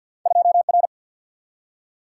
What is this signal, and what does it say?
Morse code, heard as a keyed tone, about 47 wpm, 700 Hz. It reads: 2 R
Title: Single-Letter Single-Number 50wpm